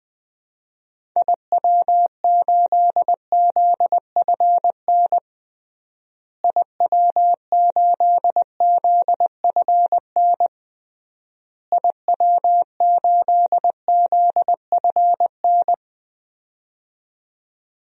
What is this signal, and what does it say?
Morse code, heard as a keyed tone, about 21 wpm, 700 Hz. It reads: IW8ZFN IW8ZFN IW8ZFN
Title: DXCC-100 Call Signs - 6 Characters - Repeated 3x - 20wpm